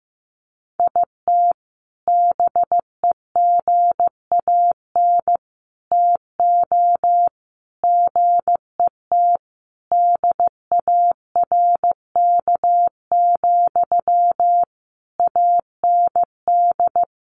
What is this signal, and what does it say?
Morse code, heard as a keyed tone, about 15 wpm, 700 Hz. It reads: IT BEGAN TO GET DARK, AND
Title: Sentences from Top 1400 Words 15wpm